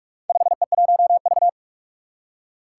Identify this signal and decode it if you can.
Morse code, heard as a keyed tone, about 44 wpm, 700 Hz. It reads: 5E1V
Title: Four Challenging Letter-Number Combinations 45wpm